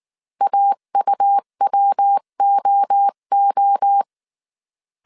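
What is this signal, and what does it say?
Morse code, heard as a keyed tone, about 19 wpm, 800 Hz. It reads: AUWOO
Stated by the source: Morse code dictation at 18 wpm from the IW3SGT QRS Morse code training course, five-character groups of 300 characters.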